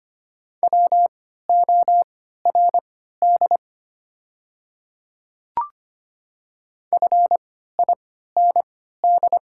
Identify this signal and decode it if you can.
Morse code, heard as a keyed tone, about 25 wpm, spaced as for 14 wpm, 700 Hz. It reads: WORD FIND
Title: 3x Character Spacing - Top 100 Words - 25wpm